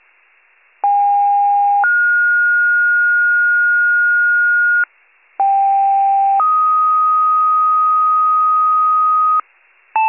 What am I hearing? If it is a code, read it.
Two-tone page, heard as a one-second tone, then a three-second tone, about 810 then 1430 Hz, about 790 then 1250 Hz, with noise. It is heard from 0.8 to 9.4 s.